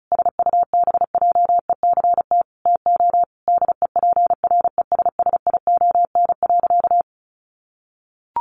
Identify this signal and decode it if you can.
Morse code, heard as a keyed tone, about 34 wpm, 700 Hz. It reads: SUBJECT TO DEPRESSION.
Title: Sentences from Top 1300 Words 35wpm